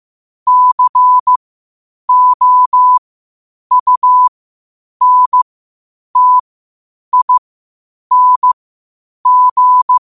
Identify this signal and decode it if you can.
Morse code, heard as a keyed tone, about 15 wpm, spaced as for 8 wpm, 1000 Hz. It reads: COUNTING
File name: qotdfiles / 8wpm